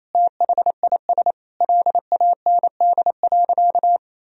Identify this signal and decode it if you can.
Morse code, heard as a keyed tone, about 28 wpm, 700 Hz. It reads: THIS LAND.
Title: Sentences from Top 1000 Words - Review Part 4 - 28wpm